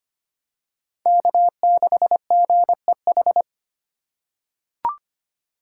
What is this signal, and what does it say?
Morse code, heard as a keyed tone, about 25 wpm, 700 Hz. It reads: K6GEH
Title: USA Call Signs - 3x Character Spacing - 25wpm